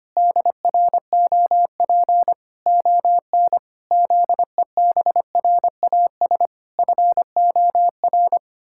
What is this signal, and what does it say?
Morse code, heard as a keyed tone, about 25 wpm, 700 Hz. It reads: DROP ON ZEBRAS FOR
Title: Pangram Challenge v2 - 25wpm